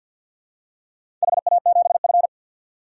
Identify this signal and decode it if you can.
Morse code, heard as a keyed tone, about 52 wpm, 700 Hz. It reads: HA7F
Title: DXCC-100 Call Signs - 4 Characters - 50wpm